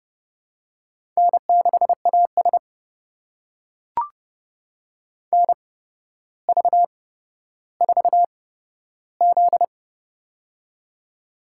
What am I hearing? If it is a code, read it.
Morse code, heard as a keyed tone, about 30 wpm, spaced as for 13 wpm, 700 Hz. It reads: N6AS NV4Z
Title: USA Call Signs - 4 Characters - 8x Character Spacing - 30wpm